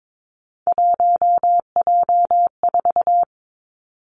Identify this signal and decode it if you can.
Morse code, heard as a keyed tone, about 22 wpm, 700 Hz. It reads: 1J4
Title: Three Challenging Letter-Number Combinations 22wpm